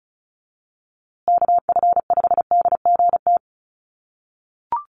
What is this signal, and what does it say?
Morse code, heard as a keyed tone, about 34 wpm, 700 Hz. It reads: KF5DGT
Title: USA Call Signs Repeated 3x 35wpm